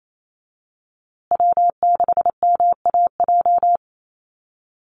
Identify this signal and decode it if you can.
Morse code, heard as a keyed tone, about 28 wpm, 700 Hz. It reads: W6MAJ
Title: USA Call Signs Repeated 3x 28wpm